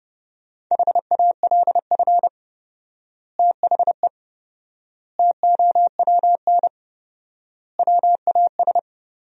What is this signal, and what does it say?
Morse code, heard as a keyed tone, about 28 wpm, 700 Hz. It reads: HALF THE TOWN WAS
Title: 4x Word-Spacing - Sentences from Top 500 Words - 30wpm